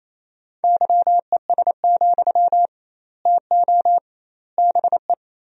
Morse code, heard as a keyed tone, about 28 wpm, 700 Hz. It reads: YES, TO BE
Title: 2x Word-Spacing - Sentences from Top 500 Words - 28wpm